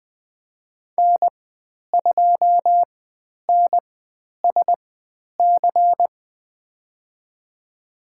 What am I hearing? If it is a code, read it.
Morse code, heard as a keyed tone, about 20 wpm, spaced as for 10 wpm, 700 Hz. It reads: N2NSC
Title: 102 Lesson 17g - Letter u - Call Signs 20wpm